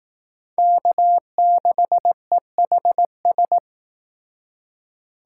Morse code, heard as a keyed tone, about 18 wpm, 700 Hz. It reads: K6EHS
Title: Speed Racing - USA 1x3 Call Signs 15wpm